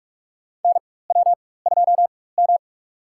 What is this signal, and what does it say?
Morse code, heard as a keyed tone, about 47 wpm, spaced as for 21 wpm, 700 Hz. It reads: NW2M
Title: USA Call Signs - 4 Characters - 4x Character Spacing - 45wpm